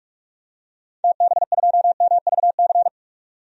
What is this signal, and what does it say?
Morse code, heard as a keyed tone, about 41 wpm, 700 Hz. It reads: TB2MVC
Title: DXCC-100 Call Signs - 6 Characters - 45wpm